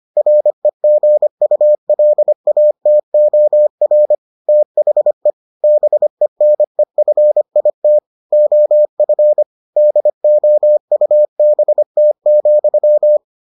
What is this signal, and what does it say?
Morse code, heard as a keyed tone, about 25 wpm, 600 Hz. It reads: REGULATOR THE BENEFIT OF DOUBT,